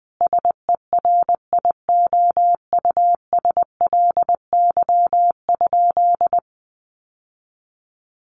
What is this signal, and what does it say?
Morse code, heard as a keyed tone, about 20 wpm, 700 Hz. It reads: SERIOUSLY?
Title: Sentences from Top 1200 Words 20wpm